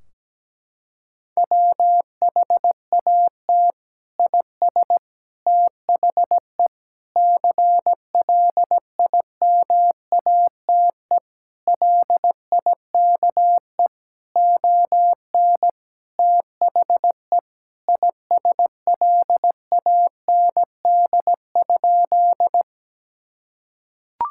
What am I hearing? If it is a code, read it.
Morse code, heard as a keyed tone, about 17 wpm, 700 Hz. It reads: WHAT IS THE CLIMATE LIKE ON THE ISLAND?
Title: Sentences from Top 1600 Words 17wpm